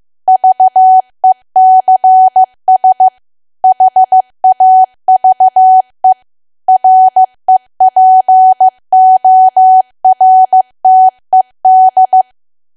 Morse code, heard as a keyed tone, about 15 wpm, 750 Hz. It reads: VECS HAVE REPORTED
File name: cwa / W1AW-Level2 / 15WPM